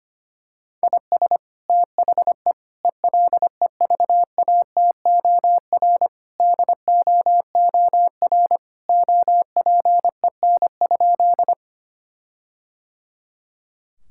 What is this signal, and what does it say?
Morse code, heard as a keyed tone, about 25 wpm, 700 Hz. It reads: IS THE ELEVATOR DOOR OPEN?
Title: Sentences from Top 1600 Words 25wpm